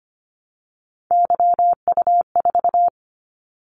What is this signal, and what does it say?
Morse code, heard as a keyed tone, about 25 wpm, 700 Hz. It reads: YU4
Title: Callsign Prefixes for Top 25 DXCC - 25wpm